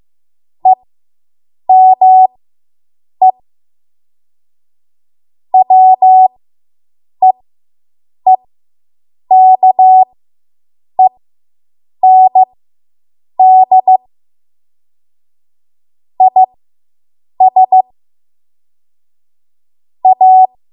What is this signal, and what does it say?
Morse code, heard as a keyed tone, about 15 wpm, spaced as for 7 wpm, 750 Hz. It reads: EME WEEKEND IS A